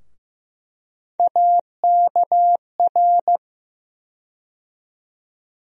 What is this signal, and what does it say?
Morse code, heard as a keyed tone, about 15 wpm, 700 Hz. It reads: AKR
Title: USA Call Sign Postfixes - 3 Letters 15wpm